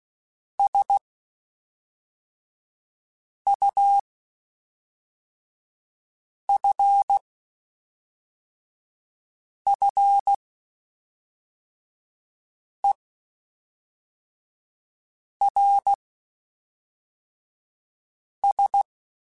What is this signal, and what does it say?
Morse code, heard as a keyed tone, about 16 wpm, spaced as for 3 wpm, 800 Hz. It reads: SUFFERS
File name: Quotes Spaced / Quotes Spaced-mp3s / HoofInMouth sp @5